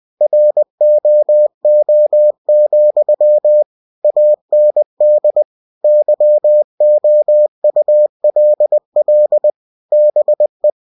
Morse code, heard as a keyed tone, about 20 wpm, 600 Hz. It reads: ROO, AND YOULL BE
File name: winnie-20wpm